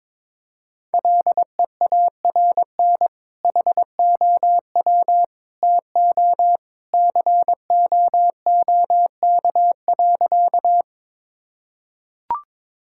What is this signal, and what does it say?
Morse code, heard as a keyed tone, about 22 wpm, 700 Hz. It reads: LEARN HOW TO COOK.
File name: practice / 22wpm / Sentences with 2-Word Phrases - 22wpm